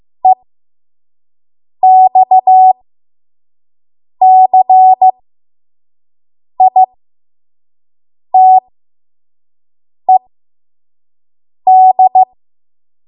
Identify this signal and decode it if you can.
Morse code, heard as a keyed tone, about 15 wpm, spaced as for 5 wpm, 750 Hz. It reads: EXCITED
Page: 5 WPM Code Archive